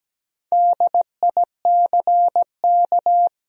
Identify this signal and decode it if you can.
Morse code, heard as a keyed tone, about 17 wpm, 700 Hz. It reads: DICK
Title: USA Common Names - Max 5 Letters - 17wpm